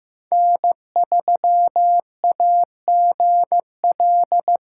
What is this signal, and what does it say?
Morse code, heard as a keyed tone, about 16 wpm, 700 Hz. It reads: N3AGL
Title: USA Callsign DE Callsign - 5 Characters - 15wpm